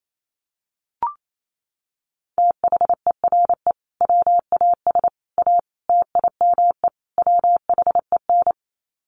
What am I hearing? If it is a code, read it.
Morse code, heard as a keyed tone, about 28 wpm, 700 Hz. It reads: THERE WAS A TIME WHEN
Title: Sets of 5 Words 28wpm